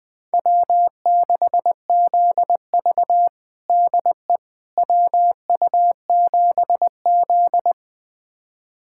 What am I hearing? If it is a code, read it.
Morse code, heard as a keyed tone, about 20 wpm, 700 Hz. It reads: W6ZV DE WU7Z
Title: USA Callsign DE Callsign - 4 Characters - 20wpm